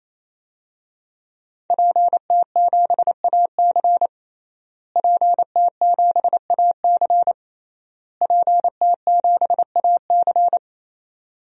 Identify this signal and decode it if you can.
Morse code, heard as a keyed tone, about 28 wpm, 700 Hz. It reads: PT7AC PT7AC PT7AC
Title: DXCC-100 Call Signs - 5 Characters - Repeated 3x - 28wpm